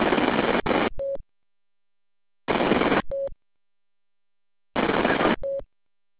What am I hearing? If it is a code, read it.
There is no code.